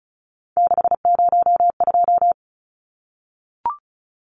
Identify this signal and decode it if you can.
Morse code, heard as a keyed tone, about 34 wpm, 700 Hz. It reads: 602
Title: Numbers - 3-Digit 35wpm